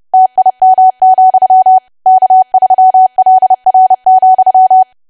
Morse code, heard as a keyed tone, about 30 wpm, 750 Hz. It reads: TIM, K3LR,
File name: cwa / W1AW-Level3 / 30WPM